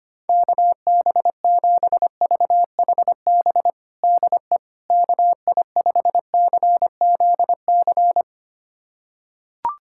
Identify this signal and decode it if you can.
Morse code, heard as a keyed tone, about 25 wpm, 700 Hz. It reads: KB7VHB DE KI5CZC
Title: USA Callsign DE Callsign - 6 Characters - 25wpm